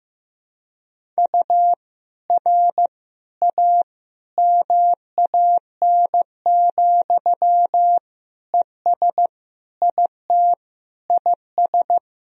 Morse code, heard as a keyed tone, about 15 wpm, 700 Hz. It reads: U R A MAN, ES IT IS
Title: Sentences from Top 200 Words and Abbreviations 15wpm